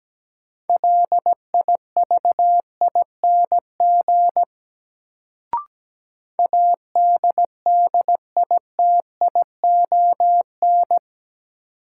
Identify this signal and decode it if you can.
Morse code, heard as a keyed tone, about 17 wpm, 700 Hz. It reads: LIVING ADDITION
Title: Sentences from Top 1000 Words - Review Part 6 - New Word Drill - 17wpm